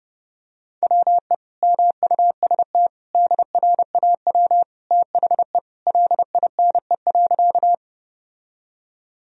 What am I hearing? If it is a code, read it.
Morse code, heard as a keyed tone, about 30 wpm, 700 Hz. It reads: WE MUST DRAW THE LINE.
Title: Sentences from Top 1000 Words - Review Part 1 - 30wpm